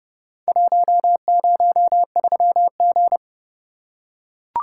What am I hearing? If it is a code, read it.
Morse code, heard as a keyed tone, about 30 wpm, 700 Hz. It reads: 103G